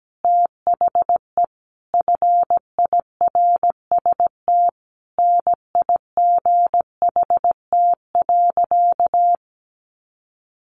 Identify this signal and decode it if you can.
Morse code, heard as a keyed tone, about 17 wpm, 700 Hz. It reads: THE FIRST NIGHT.
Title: Sentences from Top 300 Words 17wpm